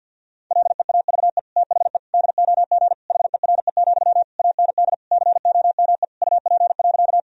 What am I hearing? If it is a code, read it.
Morse code, heard as a keyed tone, about 52 wpm, 700 Hz. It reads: LEAVE THE DOG HERE, AND COME UP.
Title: Sentences from Top 1000 Words - Review Part 2 - 50wpm